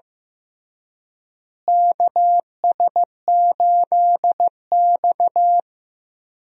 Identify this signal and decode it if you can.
Morse code, heard as a keyed tone, about 15 wpm, 700 Hz. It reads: KS8X
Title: USA Call Signs - 4 Characters - Repeated 3x - 15wpm